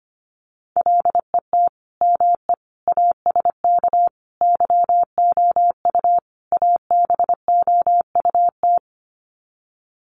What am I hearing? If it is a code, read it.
Morse code, heard as a keyed tone, about 25 wpm, 700 Hz. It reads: LET ME ASK YOU ABOUT
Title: Sets of 5 Words 25wpm